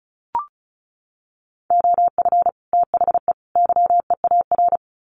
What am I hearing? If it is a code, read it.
Morse code, heard as a keyed tone, about 34 wpm, 700 Hz. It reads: OF THE YEAR
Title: Sets of 3 Words 35wpm